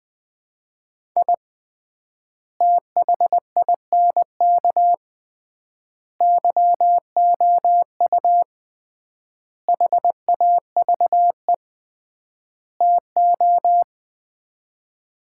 Morse code, heard as a keyed tone, about 21 wpm, 700 Hz. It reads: I THINK YOU HAVE TO
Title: 3x Word-Spacing - Sets of 5 Words - 20wpm